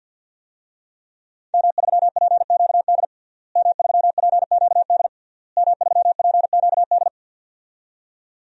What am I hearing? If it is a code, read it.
Morse code, heard as a keyed tone, about 52 wpm, 700 Hz. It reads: M3PQD M3PQD M3PQD
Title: DXCC-100 Call Signs - 5 Characters - Repeated 3x - 50wpm